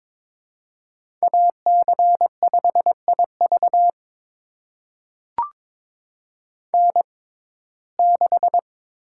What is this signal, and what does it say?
Morse code, heard as a keyed tone, about 22 wpm, spaced as for 16 wpm, 700 Hz. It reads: AC5IV N6
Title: USA Call Signs - 6x Character Spacing - 22wpm